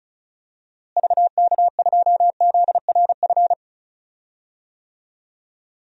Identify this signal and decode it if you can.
Morse code, heard as a keyed tone, about 36 wpm, 700 Hz. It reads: VK2ZRF
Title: DXCC-100 Call Signs - 6 Characters - Repeated 2x - 35wpm